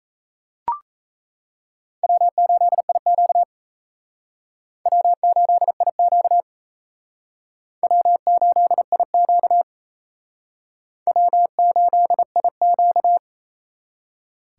Morse code, heard as a keyed tone, about 34 wpm, 700 Hz. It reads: W8IQ W8IQ W8IQ W8IQ
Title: Speed Racing - USA 1x2 Call Signs 28wpm